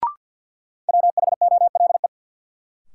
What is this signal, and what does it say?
Morse code, heard as a keyed tone, about 52 wpm, 700 Hz. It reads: WHOLE
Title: Sentences from Top 400 Words - New Word Drill - 50wpm